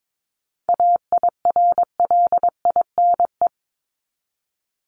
Morse code, heard as a keyed tone, about 22 wpm, 700 Hz. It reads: AIRLINE
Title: Top 1501-1750 Words 22wpm